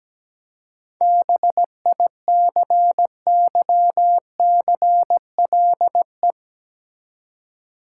Morse code, heard as a keyed tone, about 17 wpm, 700 Hz. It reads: BICYCLE
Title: Seven Letter Words - 17wpm